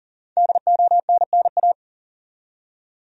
Morse code, heard as a keyed tone, about 40 wpm, 700 Hz. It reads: DONNA
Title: USA Common Names - 40wpm